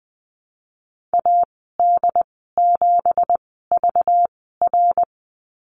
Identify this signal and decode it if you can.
Morse code, heard as a keyed tone, about 20 wpm, spaced as for 14 wpm, 700 Hz. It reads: AD7VR
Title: 193 Lesson 31g - Letter v - Call Signs 20wpm